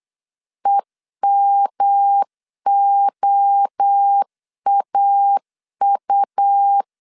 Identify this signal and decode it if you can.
Morse code, heard as a keyed tone, about 8 wpm, 800 Hz. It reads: EMOAU